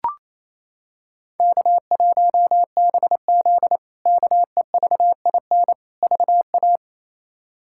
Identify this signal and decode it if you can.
Morse code, heard as a keyed tone, about 28 wpm, 700 Hz. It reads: K1BZ KEVIN VA
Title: K1USN SST QSO Practice - 28wpm